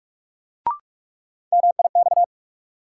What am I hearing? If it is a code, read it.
Morse code, heard as a keyed tone, about 44 wpm, 700 Hz. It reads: MIX
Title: Sentences from Top 1100 Words - New Word Drill Alt - 45wpm